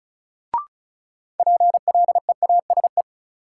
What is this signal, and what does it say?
Morse code, heard as a keyed tone, about 36 wpm, 700 Hz. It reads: PLEASE
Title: Sentences from Top 1500 Words - New Word Drill Alt - 35wpm